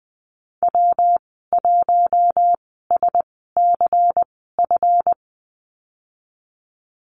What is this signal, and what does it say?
Morse code, heard as a keyed tone, about 20 wpm, spaced as for 14 wpm, 700 Hz. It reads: W1SCF
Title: 154 Lesson 25g - Letter p - Call Signs 20wpm